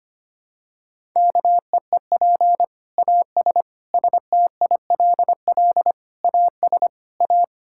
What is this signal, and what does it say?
Morse code, heard as a keyed tone, about 25 wpm, 700 Hz. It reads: KEEP AS STILL AS A
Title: Sentences from Top 1000 Words - Review Part 5 - 25wpm